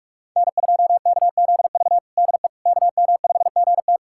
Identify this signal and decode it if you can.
Morse code, heard as a keyed tone, about 46 wpm, 700 Hz. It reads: N2KZV DE KM5GT
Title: USA Callsign DE Callsign - 5 Characters - 45wpm